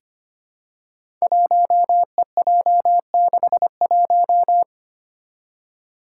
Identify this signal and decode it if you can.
Morse code, heard as a keyed tone, about 25 wpm, 700 Hz. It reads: 1EJ61
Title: Five Challenging Letter-Number Combinations 25wpm